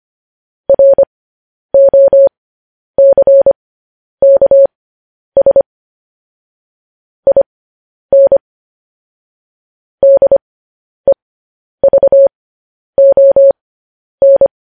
Morse code, heard as a keyed tone, about 25 wpm, spaced as for 10 wpm, 550 Hz. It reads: ROCKS IN DEVON
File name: podcast / News headlines-1x-25x10